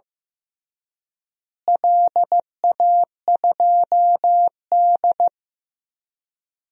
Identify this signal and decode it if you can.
Morse code, heard as a keyed tone, about 15 wpm, 700 Hz. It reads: LA2D
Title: DXCC-100 Call Signs - 4 Characters - Repeated 3x - 15wpm